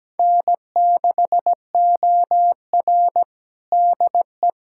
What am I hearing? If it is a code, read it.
Morse code, heard as a keyed tone, about 17 wpm, 700 Hz. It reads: N6OR DE
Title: USA Callsign DE Callsign - 4 Characters - 17wpm